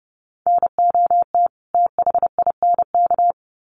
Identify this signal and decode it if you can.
Morse code, heard as a keyed tone, about 30 wpm, 700 Hz. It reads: NOT THINK
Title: Sets of 2 Words 30wpm